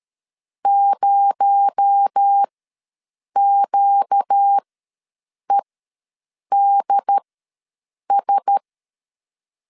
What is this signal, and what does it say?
Morse code, heard as a keyed tone, about 13 wpm, spaced as for 7 wpm, 800 Hz. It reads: 0QEDS
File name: LEZIONI / 6 PKX0 / 6 PKX0 ws 12wpm 3448